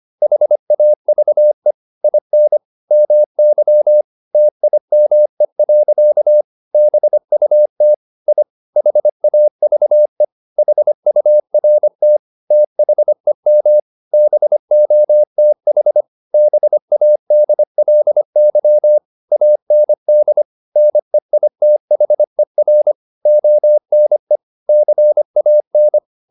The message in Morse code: HAVE IN MY TIME. BUT I HAVE HURT THEM BOTH BADLY AND NEITHER ONE CAN

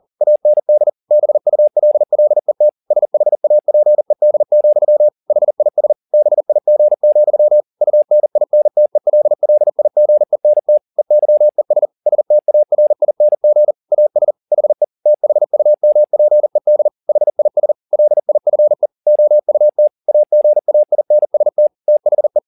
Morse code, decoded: AND BULLET SHAPED, HIS BIG, UNINTELLIGENT EYES STARING AS HE THUMPED HIS LIFE OUT AGAINST THE

